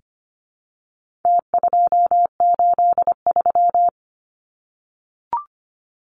Morse code, heard as a keyed tone, about 25 wpm, 700 Hz. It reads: T283
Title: Numbers - Contest Style - 4-Digit 25wpm